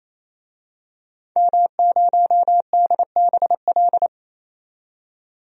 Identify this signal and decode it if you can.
Morse code, heard as a keyed tone, about 28 wpm, 700 Hz. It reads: M0DBL
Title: DXCC-100 Call Signs - 5 Characters - 28wpm